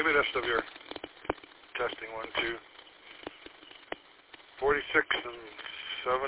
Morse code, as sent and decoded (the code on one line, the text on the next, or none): none